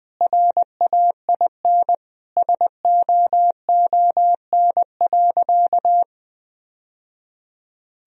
Morse code, heard as a keyed tone, about 20 wpm, 700 Hz. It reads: RAIN SOON.
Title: Sentences from Top 1000 Words - Review Part 1 - 20wpm